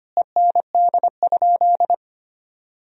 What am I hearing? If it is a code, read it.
Morse code, heard as a keyed tone, about 25 wpm, 700 Hz. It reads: END?
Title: 4x Word-Spacing - Sentences from Top 200 Words - 25wpm